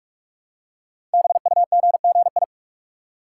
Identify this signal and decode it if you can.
Morse code, heard as a keyed tone, about 44 wpm, 690 Hz. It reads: BUGGI
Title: Code Groups - 5 Letter:Numbers - 45wpm